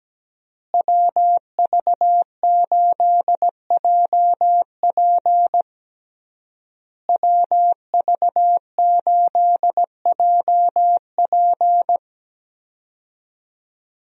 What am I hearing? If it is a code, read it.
Morse code, heard as a keyed tone, about 17 wpm, 700 Hz. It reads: WV8JP WV8JP
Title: USA Call Signs - 5 Characters - Repeated 3x - 17wpm